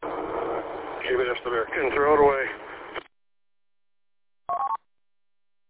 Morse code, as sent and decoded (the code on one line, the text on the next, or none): none